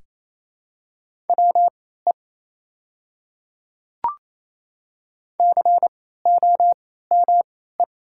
Morse code, heard as a keyed tone, about 28 wpm, spaced as for 16 wpm, 700 Hz. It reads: WE COME
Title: 3x Character Spacing - Top 100 Words - 28wpm